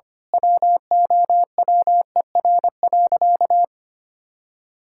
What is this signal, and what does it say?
Morse code, heard as a keyed tone, about 25 wpm, 700 Hz. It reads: WOWER.
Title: Pangram Challenge v2 - 25wpm